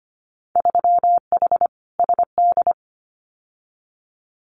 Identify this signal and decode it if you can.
Morse code, heard as a keyed tone, about 25 wpm, 700 Hz. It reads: 3H SD
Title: Winter Field Day Exchanges Encore 25wpm